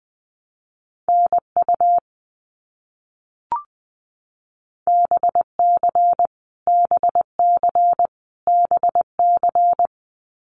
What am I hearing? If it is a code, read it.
Morse code, heard as a keyed tone, about 20 wpm, 700 Hz. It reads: NU BC BC BC
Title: Canadian Provinces and Territories - Repeated 3x - 20wpm